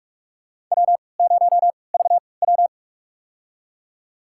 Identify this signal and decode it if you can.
Morse code, heard as a keyed tone, about 44 wpm, spaced as for 25 wpm, 690 Hz. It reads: W0VW
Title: USA Call Signs - 4 Characters - 3x Character Spacing - 45wpm